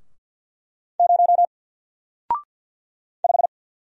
Morse code, heard as a keyed tone, about 48 wpm, 700 Hz. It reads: 0 5
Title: Single Letter-Number Alt 50wpm